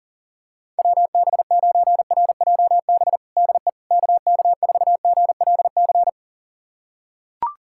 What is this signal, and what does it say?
Morse code, heard as a keyed tone, about 40 wpm, 700 Hz. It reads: WB9RJB DE KK4GLC
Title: USA Callsign DE Callsign 40wpm